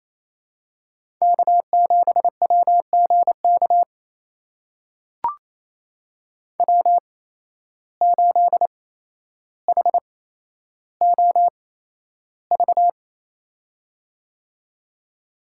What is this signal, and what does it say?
Morse code, heard as a keyed tone, about 28 wpm, spaced as for 12 wpm, 700 Hz. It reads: K7WGK W8HOV
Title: USA Call Signs - 5 Characters - 8x Character Spacing - 28wpm